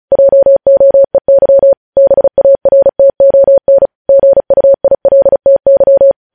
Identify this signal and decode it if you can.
Morse code, heard as a keyed tone, about 36 wpm, 550 Hz. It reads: JOEY BARTON GUILTY
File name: podcast / News headlines-1x-35x35